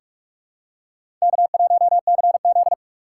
Morse code, heard as a keyed tone, about 46 wpm, 700 Hz. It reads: K1CZ
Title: USA Call Signs - 3x Character Spacing - 45wpm